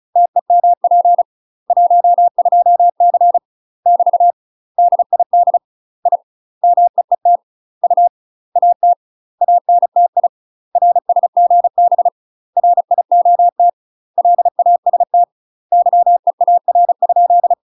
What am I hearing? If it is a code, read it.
Morse code, heard as a keyed tone, about 34 wpm, 700 Hz. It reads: TEMP 12C <BT> DID I MEET U AT ANTI RSGB RIOT LAST YEAR?
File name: UK MASTER QSOs TRAINING / 35WPM / UK MASTER QSOs 0037